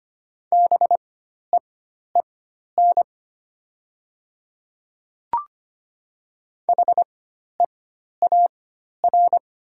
Morse code, heard as a keyed tone, about 25 wpm, spaced as for 12 wpm, 700 Hz. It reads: BEEN HEAR
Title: 4x Character Spacing - Top 500 Words - 25wpm